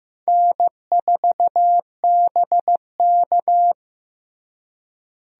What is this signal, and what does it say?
Morse code, heard as a keyed tone, about 15 wpm, 700 Hz. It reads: N4BK
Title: USA Call Signs - 4 Characters - Repeated 2x - 15wpm